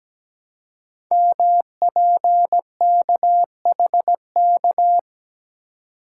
Morse code, heard as a keyed tone, about 17 wpm, 700 Hz. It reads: MPKHK
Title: Code Groups - 5 Letter:Numbers - 17wpm